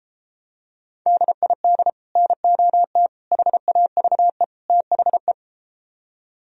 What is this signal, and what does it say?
Morse code, heard as a keyed tone, about 34 wpm, 700 Hz. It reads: DID NOT HAVE THE